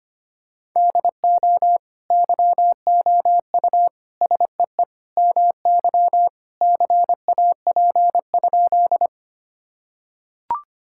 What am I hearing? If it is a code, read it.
Morse code, heard as a keyed tone, about 25 wpm, 700 Hz. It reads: DO YOU SEE MY CAP?
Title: Sentences from Top 1600 Words 25wpm